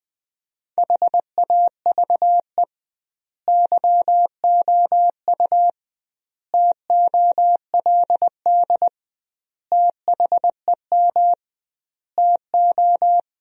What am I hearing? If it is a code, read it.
Morse code, heard as a keyed tone, about 21 wpm, 700 Hz. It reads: HAVE YOU TOLD THEM TO
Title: 2x Word-Spacing - Sentences from Top 500 Words - 20wpm